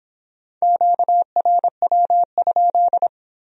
Morse code, heard as a keyed tone, about 24 wpm, 700 Hz. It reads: QRW?